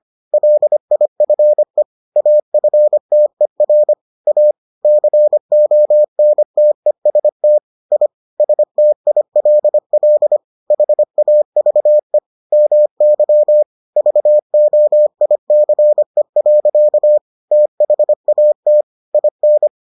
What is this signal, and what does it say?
Morse code, heard as a keyed tone, about 25 wpm, 600 Hz. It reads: LIFE AFTER A CONTEST I STILL HAVE MY VOICE. THAT IN